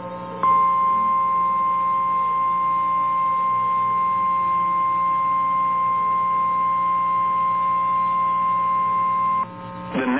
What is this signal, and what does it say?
Emergency Alert System (SAME) tones (the 1050 Hz attention tone). It is heard from 0.4 to 9.2 s.